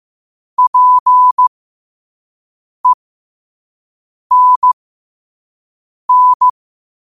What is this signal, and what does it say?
Morse code, heard as a keyed tone, about 15 wpm, spaced as for 5 wpm, 1000 Hz. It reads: PENN
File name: 5wpm